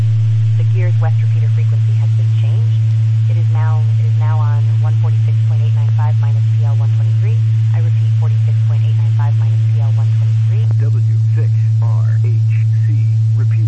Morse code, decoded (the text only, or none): none